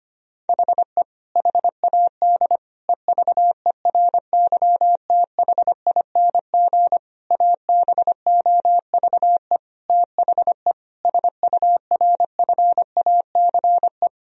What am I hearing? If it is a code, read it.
Morse code, heard as a keyed tone, about 25 wpm, 700 Hz. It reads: HE HAD EVERYTHING ABOVE THE SURFACE
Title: Sentences from Top 1000 Words - Review Part 3 - 25wpm